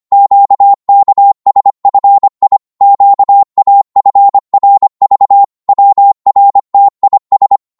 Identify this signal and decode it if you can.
Morse code, heard as a keyed tone, about 25 wpm, 800 Hz. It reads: QKSFI QAFRV WRTIS
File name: Morse / ltr-25wpm